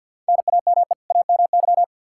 Morse code, heard as a keyed tone, about 52 wpm, 700 Hz. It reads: NAME AMY
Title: Practice QSO - Exchange 1 - Beginning - 50wpm